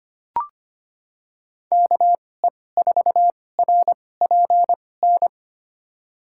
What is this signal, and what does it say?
Morse code, heard as a keyed tone, about 25 wpm, spaced as for 18 wpm, 700 Hz. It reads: KE4RPN
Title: USA Call Signs - 6 Characters - 2x Character Spacing - 25wpm